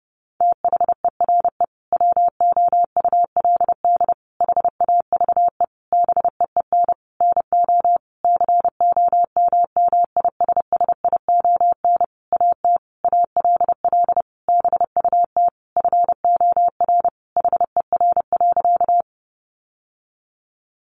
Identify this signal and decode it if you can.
Morse code, heard as a keyed tone, about 30 wpm, 700 Hz. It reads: THERE WOULD HAVE BEEN NO COMMISSION AT ALL BUT FOR HER.
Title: Sentences from Top 1400 Words 30wpm